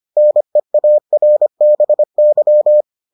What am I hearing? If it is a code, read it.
Morse code, heard as a keyed tone, about 25 wpm, 600 Hz. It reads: NEARBY